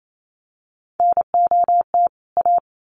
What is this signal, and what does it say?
Morse code, heard as a keyed tone, about 28 wpm, 700 Hz. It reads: NOT A
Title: Sentences from Top 1300 Words 28wpm